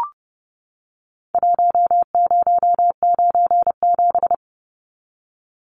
Morse code, heard as a keyed tone, about 30 wpm, 700 Hz. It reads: 1097
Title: Numbers - Contest Style - 4-Digit 30wpm